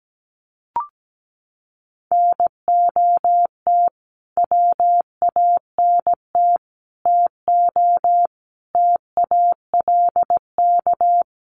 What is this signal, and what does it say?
Morse code, heard as a keyed tone, about 17 wpm, 700 Hz. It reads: NOT WANT TO TALK
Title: Sets of 5 Words 17wpm